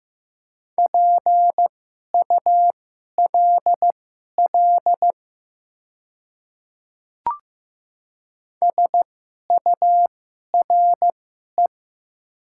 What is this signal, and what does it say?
Morse code, heard as a keyed tone, about 15 wpm, spaced as for 11 wpm, 690 Hz. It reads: PULL SURE
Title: 2x Character Spacing - Top 500 Words - 15wpm